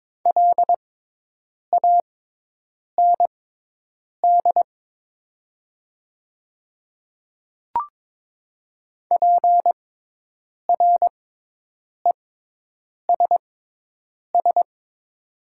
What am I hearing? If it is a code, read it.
Morse code, heard as a keyed tone, about 22 wpm, spaced as for 8 wpm, 700 Hz. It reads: LAND PRESS
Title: 6x Character Spacing - Top 300 Words - 22wpm